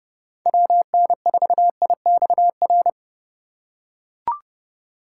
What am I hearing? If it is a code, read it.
Morse code, heard as a keyed tone, about 31 wpm, 700 Hz. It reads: WN4IXR